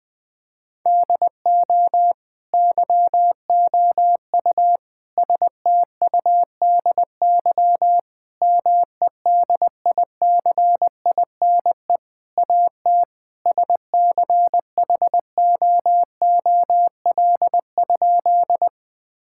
Morse code, heard as a keyed tone, about 20 wpm, 700 Hz. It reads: DO YOU STUDY MEDICINE AT SCHOOL?
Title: Sentences from Top 1000 Words - Review Part 6 - 20wpm